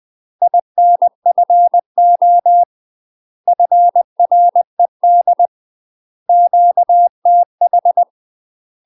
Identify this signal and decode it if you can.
Morse code, heard as a keyed tone, about 20 wpm, 700 Hz. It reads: INFO FRED QTH
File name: UK MASTER QSOs TRAINING/20WPM